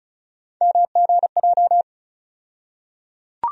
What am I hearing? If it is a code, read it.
Morse code, heard as a keyed tone, about 36 wpm, 700 Hz. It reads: MGJ
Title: USA Call Sign Postfixes - 3 Letters 35wpm